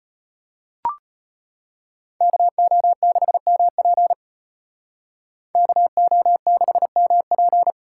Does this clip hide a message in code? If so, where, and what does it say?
Yes: Morse code from 2.2 to 7.7 s, heard as a keyed tone, about 32 wpm, 700 Hz. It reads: KO6MP KO6MP